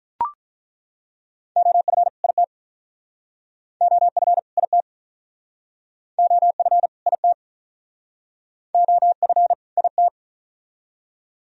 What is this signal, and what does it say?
Morse code, heard as a keyed tone, about 48 wpm, 700 Hz. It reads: OF IT OF IT OF IT OF IT